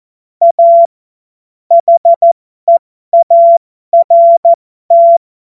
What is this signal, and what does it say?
Morse code, heard as a keyed tone, about 14 wpm, spaced as for 12 wpm, 680 Hz. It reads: A HEART
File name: Morsekurs / Witze